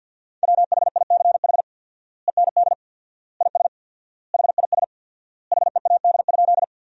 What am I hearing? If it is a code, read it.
Morse code, heard as a keyed tone, about 50 wpm, 700 Hz. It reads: WHICH END IS HIS HEAD?
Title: 4x Word-Spacing - Sentences from Top 300 Words - 50wpm